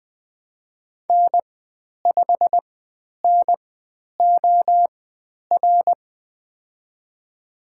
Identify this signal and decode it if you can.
Morse code, heard as a keyed tone, about 20 wpm, spaced as for 10 wpm, 700 Hz. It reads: N5NOR
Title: 095 Lesson 16g - Letter c - Call Signs 20wpm